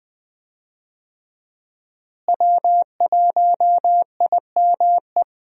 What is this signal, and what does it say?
Morse code, heard as a keyed tone, about 20 wpm, 700 Hz. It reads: W1IME
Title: USA Call Signs - 5 Characters - Repeated 2x - 20wpm